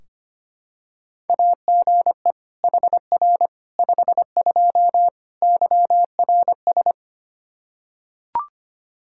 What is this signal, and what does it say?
Morse code, heard as a keyed tone, about 25 wpm, 700 Hz. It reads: AGE HR 52 YRS
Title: Practice QSO - Exchange 3 - More Info - 25wpm